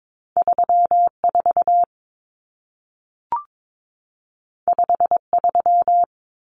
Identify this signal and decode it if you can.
Morse code, heard as a keyed tone, about 22 wpm, 700 Hz. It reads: 34 53
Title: Numbers - 2-Digit 22wpm